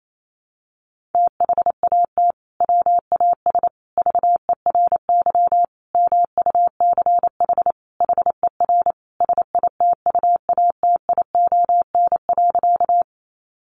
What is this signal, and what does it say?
Morse code, heard as a keyed tone, about 28 wpm, 700 Hz. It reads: THAT WAS VERY MUCH HER SITUATION.
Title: Sentences from Top 1200 Words 28wpm